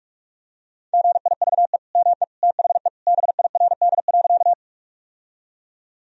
Morse code, heard as a keyed tone, about 46 wpm, 700 Hz. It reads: GIVE ME THE BIRD.
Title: Sentences from Top 1000 Words - Review Part 2 - 45wpm